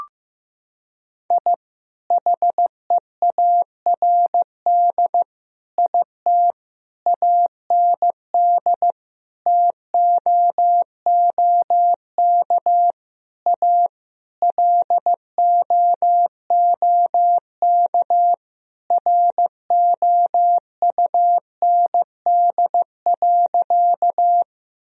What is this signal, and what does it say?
Morse code, heard as a keyed tone, about 15 wpm, 700 Hz. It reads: I HEARD IT AND TOOK A LOOK ROUND.
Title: Sentences from Top 1000 Words - Review Part 1 - 15wpm